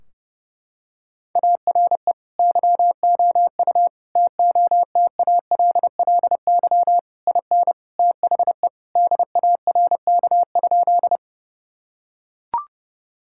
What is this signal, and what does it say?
Morse code, heard as a keyed tone, about 30 wpm, 700 Hz. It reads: ARE YOU TOTALLY IN THE DARK?
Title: Sentences from Top 1600 Words 30wpm